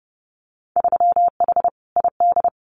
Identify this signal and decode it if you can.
Morse code, heard as a keyed tone, about 32 wpm, 700 Hz. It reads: 3H ID
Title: Winter Field Day Exchanges Encore 30wpm